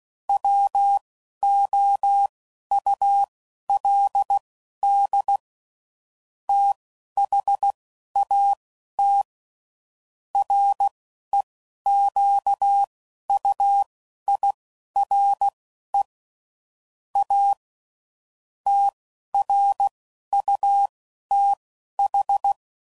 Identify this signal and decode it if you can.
Morse code, heard as a keyed tone, about 16 wpm, spaced as for 11 wpm, 800 Hz. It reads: WOULD THAT REQUIRE A TRUTH
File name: CW Code Groups @10WPM/Technical Subjects